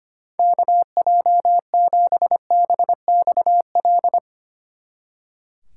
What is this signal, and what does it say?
Morse code, heard as a keyed tone, about 32 wpm, 700 Hz. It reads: KJ7BXL